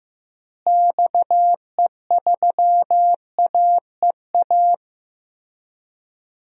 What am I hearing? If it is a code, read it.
Morse code, heard as a keyed tone, about 16 wpm, 700 Hz. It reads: XE3AEA
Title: DXCC-100 Call Signs - 6 Characters - Repeated 3x - 15wpm